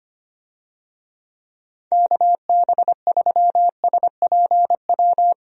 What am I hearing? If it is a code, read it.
Morse code, heard as a keyed tone, about 25 wpm, 700 Hz. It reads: KB3SPW
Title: USA Call Signs - 6 Characters - Repeated 3x - 25wpm